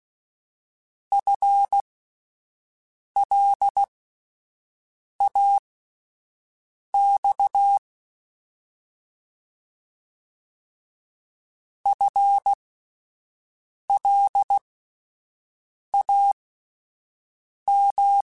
Morse code, heard as a keyed tone, about 16 wpm, spaced as for 5 wpm, 800 Hz. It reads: FLAX FLAM
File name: Alphabetical / Alphabetical mp3s / Fs @5